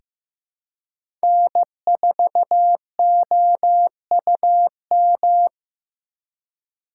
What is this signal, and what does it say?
Morse code, heard as a keyed tone, about 15 wpm, 700 Hz. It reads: N4OUM
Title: USA Call Signs - 4x Character Spacing - 15wpm